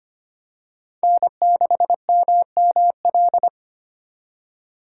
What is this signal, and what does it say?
Morse code, heard as a keyed tone, about 25 wpm, 700 Hz. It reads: N6MML
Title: USA Call Signs - 5 Characters - 3x Character Spacing - 25wpm